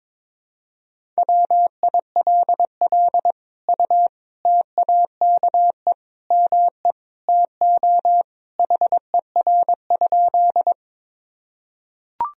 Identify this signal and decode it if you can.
Morse code, heard as a keyed tone, about 22 wpm, 700 Hz. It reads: WILL U TAKE ME TO HER?
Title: Sentences from Top 200 Words and Abbreviations 22wpm